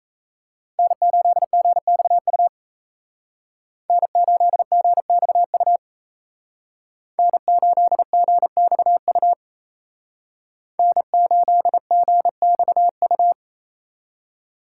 Morse code, heard as a keyed tone, about 36 wpm, 700 Hz. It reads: N8GXU N8GXU N8GXU N8GXU